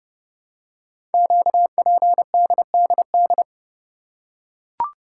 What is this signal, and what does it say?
Morse code, heard as a keyed tone, about 30 wpm, 700 Hz. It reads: QPDDD
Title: Code Groups - 5 Letters - 30wpm